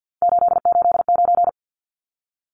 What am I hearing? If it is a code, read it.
Morse code, heard as a keyed tone, about 50 wpm, 700 Hz. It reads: 889
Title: Numbers - 3-Digit 50wpm